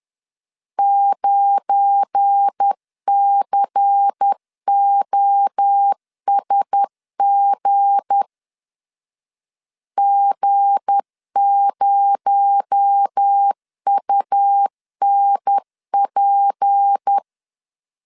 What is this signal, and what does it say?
Morse code, heard as a keyed tone, about 10 wpm, 800 Hz. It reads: 9COSG G0UNP
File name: LEZIONI / 7 GF9Z / 7 GF9 10wpm 2747